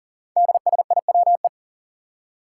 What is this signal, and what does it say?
Morse code, heard as a keyed tone, about 39 wpm, 700 Hz. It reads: DSIWE
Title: Five Challenging Letter-Number Combinations 40wpm